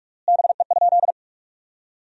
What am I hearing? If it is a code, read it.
Morse code, heard as a keyed tone, about 46 wpm, 700 Hz. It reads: BE?
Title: 3x Word-Spacing - Sentences from Top 100 Words - 45wpm